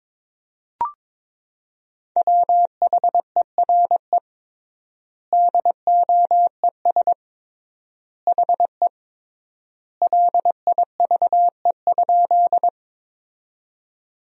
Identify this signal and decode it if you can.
Morse code, heard as a keyed tone, about 22 wpm, 700 Hz. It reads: WHERE DOES HE LIVE?
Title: 3x Word-Spacing - Sentences from Top 200 Words - 22wpm